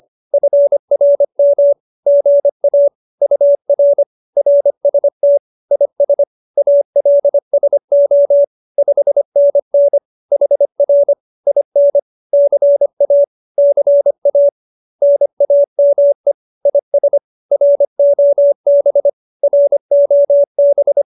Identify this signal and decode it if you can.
Morse code, heard as a keyed tone, about 25 wpm, 570 Hz. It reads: FRM GA UR RST IS ALSO 5NN HR IN CA CA NAME IS ROB ROB